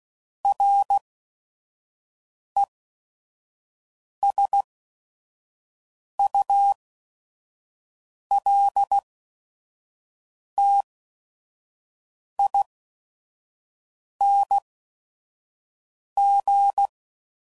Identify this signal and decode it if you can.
Morse code, heard as a keyed tone, about 16 wpm, spaced as for 5 wpm, 800 Hz. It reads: RESULTING
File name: CW Code Groups @5WPM / Technical Subjects / XL